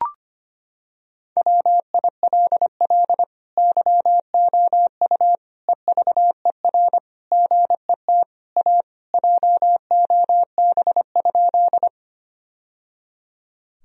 Morse code, heard as a keyed tone, about 25 wpm, 700 Hz. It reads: WILL YOU EVER GET A JOB?
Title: Sentences from Top 1000 Words - Review Part 3 - 25wpm